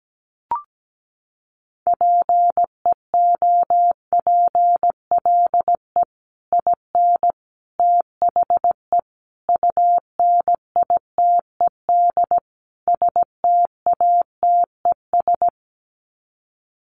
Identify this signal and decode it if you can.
Morse code, heard as a keyed tone, about 17 wpm, 700 Hz. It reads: PEOPLE IN THE UNITED STATES
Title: Sets of 5 Words Encore 17wpm